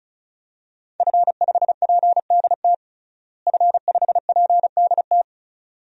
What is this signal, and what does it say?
Morse code, heard as a keyed tone, about 34 wpm, 700 Hz. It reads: F5PDT F5PDT
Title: DXCC-100 Call Signs - 5 Characters - Repeated 2x - 35wpm